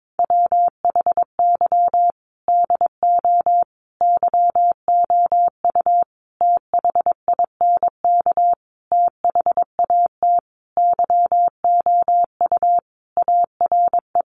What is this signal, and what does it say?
Morse code, heard as a keyed tone, about 22 wpm, 700 Hz. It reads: WHY DO YOU THINK THAT YOU ARE
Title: Sentences from Top 1400 Words 22wpm